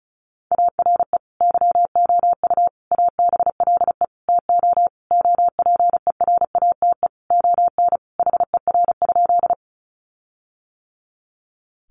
Morse code, heard as a keyed tone, about 34 wpm, 700 Hz. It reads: ARE YOU ABLE TO OPERATE ON HER?
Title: Sentences from Top 1500 Words 35wpm